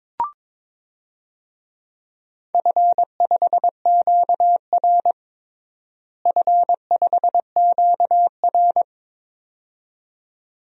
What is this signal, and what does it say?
Morse code, heard as a keyed tone, about 22 wpm, 700 Hz. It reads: F5QR F5QR
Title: DXCC-100 Call Signs - 4 Characters - Repeated 2x - 22wpm